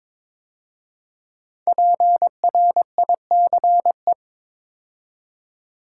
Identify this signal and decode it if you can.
Morse code, heard as a keyed tone, about 22 wpm, 700 Hz. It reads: PRICE
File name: 5 Letter Words - First 3 Letters Given - 22wpm